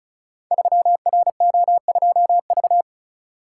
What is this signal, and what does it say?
Morse code, heard as a keyed tone, about 36 wpm, 700 Hz. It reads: 3RO2V
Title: Code Groups - 5 Letter:Numbers - 35wpm